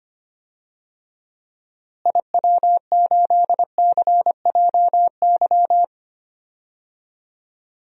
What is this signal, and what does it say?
Morse code, heard as a keyed tone, about 25 wpm, 700 Hz. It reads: IW8CJY
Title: DXCC-100 Call Signs - 6 Characters - Repeated 2x - 25wpm